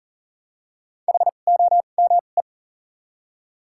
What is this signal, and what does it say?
Morse code, heard as a keyed tone, about 40 wpm, spaced as for 29 wpm, 700 Hz. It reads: HOME
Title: 2x Character Spacing - Top 300 Words - 40wpm